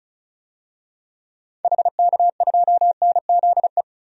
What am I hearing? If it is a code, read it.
Morse code, heard as a keyed tone, about 36 wpm, 700 Hz. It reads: HK2NZE
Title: DXCC-100 Call Signs - 6 Characters - 35wpm